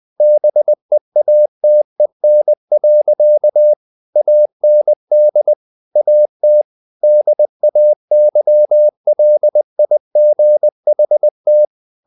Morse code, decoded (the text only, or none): BEATEN. AND AT DAYLIGHT